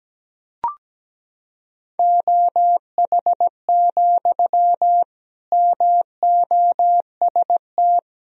Morse code, heard as a keyed tone, about 17 wpm, 700 Hz. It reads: OH, MOST
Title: Sentences from Top 1000 Words - Review Part 1 - 17wpm